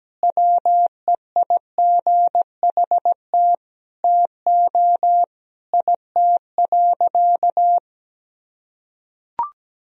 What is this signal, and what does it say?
Morse code, heard as a keyed tone, about 17 wpm, 700 Hz. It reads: WEIGHT TO IT.
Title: Sentences from Top 1000 Words - Review Part 3 - 17wpm